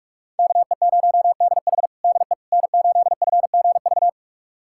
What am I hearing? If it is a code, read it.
Morse code, heard as a keyed tone, about 47 wpm, 700 Hz. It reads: KE0DH DE N8FGV
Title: USA Callsign DE Callsign - 5 Characters - 45wpm